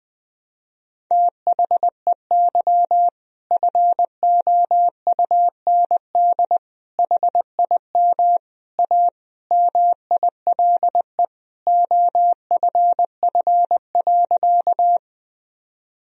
Morse code, heard as a keyed tone, about 21 wpm, 700 Hz. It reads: THEY FOUND HIM A MILE OFF.
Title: Sentences from Top 1000 Words - Review Part 1 - 20wpm